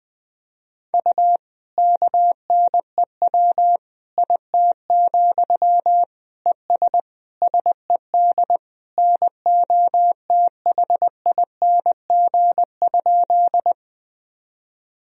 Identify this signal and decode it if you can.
Morse code, heard as a keyed tone, about 20 wpm, 700 Hz. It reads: U KNEW IT, ES SED NOTHING?
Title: Sentences from Top 500 Words and Abbreviations 20wpm